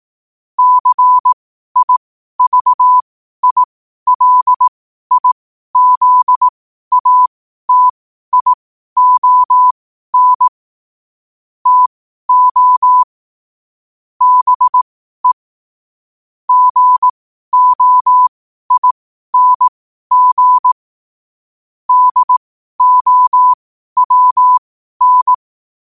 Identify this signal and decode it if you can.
Morse code, heard as a keyed tone, about 18 wpm, spaced as for 12 wpm, 1000 Hz. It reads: CIVILIZATION TO BE GOING DOWN
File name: qotdfiles / 12wpm